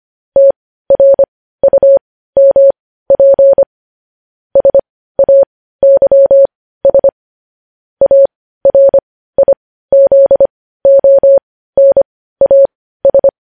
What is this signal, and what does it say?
Morse code, heard as a keyed tone, about 25 wpm, spaced as for 15 wpm, 550 Hz. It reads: TRUMP SAYS ARIZONAS
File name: podcast / News headlines-1x-25x15